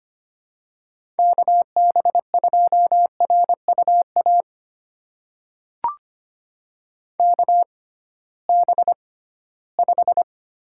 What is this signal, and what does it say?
Morse code, heard as a keyed tone, about 25 wpm, spaced as for 16 wpm, 700 Hz. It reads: KB2RUA KB5
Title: USA Call Signs - 6 Characters - 6x Character Spacing - 25wpm